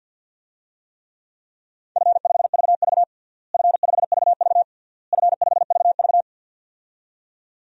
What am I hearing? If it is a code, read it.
Morse code, heard as a keyed tone, about 48 wpm, 700 Hz. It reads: F5VV F5VV F5VV
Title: DXCC-100 Call Signs - 4 Characters - Repeated 3x - 50wpm